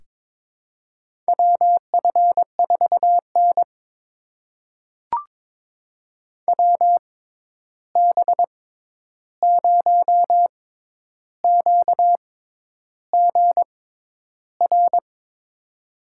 Morse code, heard as a keyed tone, about 22 wpm, spaced as for 10 wpm, 700 Hz. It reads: WF4N WB0QGR
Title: USA Call Signs - 6x Character Spacing - 22wpm